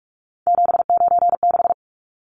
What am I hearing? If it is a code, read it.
Morse code, heard as a keyed tone, about 44 wpm, 700 Hz. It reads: 796
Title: Numbers - 3-Digit 45wpm